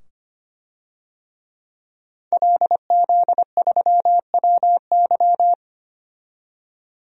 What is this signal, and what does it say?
Morse code, heard as a keyed tone, about 25 wpm, 700 Hz. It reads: LZ3WY
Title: DXCC-100 Call Signs - 5 Characters - 25wpm